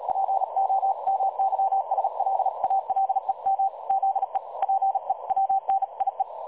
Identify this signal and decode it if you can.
Morse code, heard as a keyed tone, about 38 wpm, 780 Hz. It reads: POMYSLALEM ZE PEWNIE
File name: dlazaawansowanych / nagraniazpasma / websdr recording 3565.0kHz